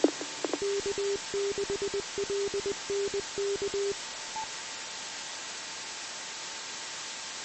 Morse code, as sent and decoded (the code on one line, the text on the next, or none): none